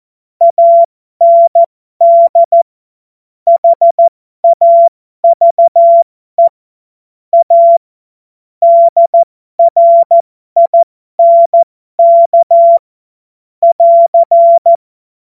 Morse code, heard as a keyed tone, about 14 wpm, spaced as for 12 wpm, 680 Hz. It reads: AND HAVE A DRINK <AR>
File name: Morsekurs / Witze